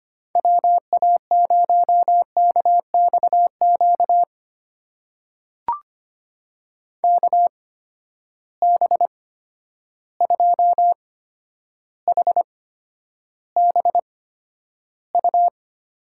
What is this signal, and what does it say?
Morse code, heard as a keyed tone, about 25 wpm, spaced as for 11 wpm, 700 Hz. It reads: WA0KXQ KB2HBU
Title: USA Call Signs - 6 Characters - 8x Character Spacing - 25wpm